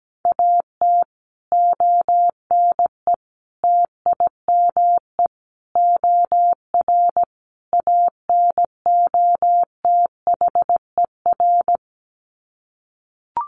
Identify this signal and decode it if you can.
Morse code, heard as a keyed tone, about 17 wpm, 700 Hz. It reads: AT ONE TIME OR ANOTHER
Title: Sets of 5 Words Encore 17wpm